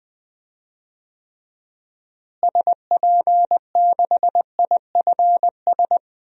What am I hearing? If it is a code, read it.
Morse code, heard as a keyed tone, about 20 wpm, 700 Hz. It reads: SP6IFS
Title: DXCC-100 Call Signs - 6 Characters - Repeated 2x - 20wpm